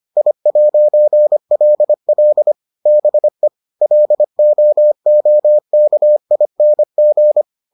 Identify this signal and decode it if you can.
Morse code, heard as a keyed tone, about 25 wpm, 600 Hz. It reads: I'LL BE LOOKING